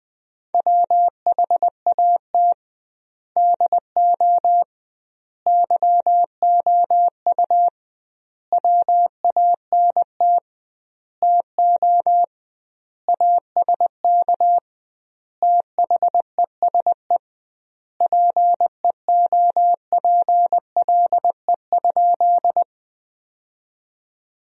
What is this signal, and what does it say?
Morse code, heard as a keyed tone, about 21 wpm, 700 Hz. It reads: WHAT DO YOU WANT TO ASK THESE PEOPLE?
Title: 2x Word-Spacing - Sentences from Top 200 Words - 20wpm